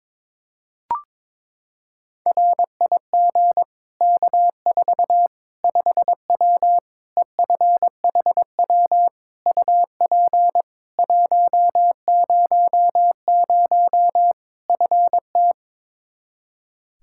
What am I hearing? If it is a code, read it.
Morse code, heard as a keyed tone, about 22 wpm, 700 Hz. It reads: RIG K4 5W EFHW UP 100 FT
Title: Practice QSO - Exchange 2 - Info - 22wpm